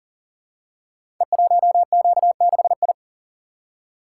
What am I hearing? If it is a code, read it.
Morse code, heard as a keyed tone, about 39 wpm, 700 Hz. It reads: E1Q6I
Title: Code Groups - 5 Letter:Numbers - 40wpm